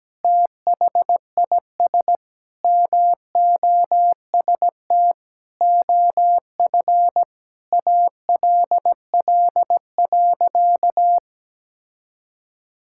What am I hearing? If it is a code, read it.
Morse code, heard as a keyed tone, about 17 wpm, 700 Hz. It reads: THIS MOST OF ALL.
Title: Sentences from Top 1000 Words - Review Part 6 - 17wpm